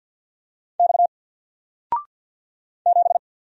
Morse code, heard as a keyed tone, about 48 wpm, 700 Hz. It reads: X 7
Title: Single Letter-Number Alt 50wpm